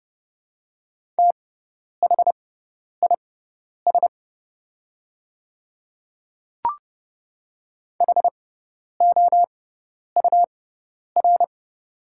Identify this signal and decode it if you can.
Morse code, heard as a keyed tone, about 30 wpm, spaced as for 10 wpm, 690 Hz. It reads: THIS HOUR
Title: 6x Character Spacing - Top 500 Words - 30wpm